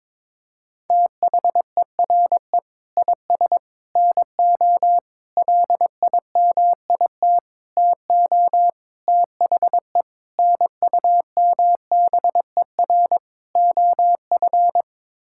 Morse code, heard as a keyed tone, about 22 wpm, 700 Hz. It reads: THERE IS NO LIMIT TO THE NUMBER OF
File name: practice / 22wpm / Sets of 8 Words - 22wpm